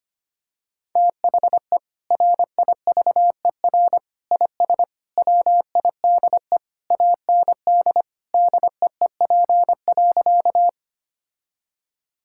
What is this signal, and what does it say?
Morse code, heard as a keyed tone, about 25 wpm, 700 Hz. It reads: THE RIVER IS WIDE AND DEEP.
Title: Sentences from Top 1000 Words - Review Part 6 - 25wpm